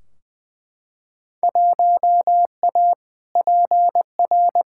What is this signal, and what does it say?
Morse code, heard as a keyed tone, about 20 wpm, 700 Hz. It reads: 1A PR
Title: Field Day Exchanges Encore - 20wpm